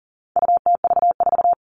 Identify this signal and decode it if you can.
Morse code, heard as a keyed tone, about 37 wpm, 700 Hz. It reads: UTV4
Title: All Cut Numbers - 4-Digit 40wpm